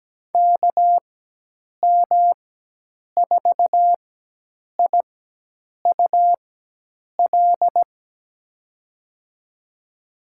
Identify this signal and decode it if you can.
Morse code, heard as a keyed tone, about 17 wpm, 700 Hz. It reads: K M 4 I U L
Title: USA Call Signs - 6 Characters - 4x Character Spacing - 17wpm